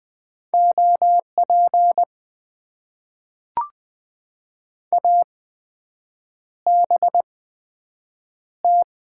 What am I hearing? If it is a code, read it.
Morse code, heard as a keyed tone, about 20 wpm, spaced as for 7 wpm, 700 Hz. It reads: OP ABT
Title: Top 90 QSO Elements - 8x Character Spacing - 20wpm